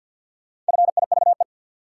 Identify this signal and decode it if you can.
Morse code, heard as a keyed tone, about 53 wpm, 700 Hz. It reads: FIVE
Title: Sentences from Top 400 Words - New Word Drill - 50wpm